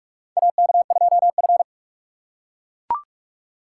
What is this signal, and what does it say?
Morse code, heard as a keyed tone, about 41 wpm, 700 Hz. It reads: AK2F
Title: USA Call Signs - 4 Characters - 2x Character Spacing - 45wpm